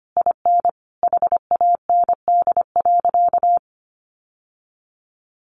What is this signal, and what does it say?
Morse code, heard as a keyed tone, about 25 wpm, 700 Hz. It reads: IN HAND.
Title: Sentences from Top 1100 Words 25wpm